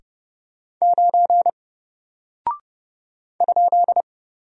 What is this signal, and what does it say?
Morse code, heard as a keyed tone, about 31 wpm, 700 Hz. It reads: 9 ?
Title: Single Character (Letter:Number:Punctuation:Prosign) Alt 30wpm